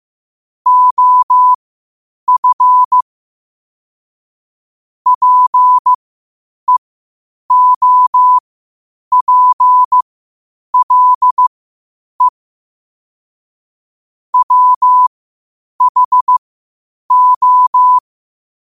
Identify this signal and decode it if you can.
Morse code, heard as a keyed tone, about 15 wpm, spaced as for 8 wpm, 1000 Hz. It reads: OF PEOPLE WHO
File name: qotdfiles / 8wpm